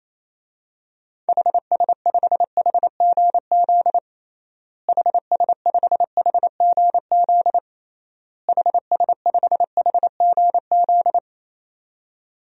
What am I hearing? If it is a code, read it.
Morse code, heard as a keyed tone, about 28 wpm, 700 Hz. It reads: HS5HGZ HS5HGZ HS5HGZ
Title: DXCC-100 Call Signs - 6 Characters - Repeated 3x - 28wpm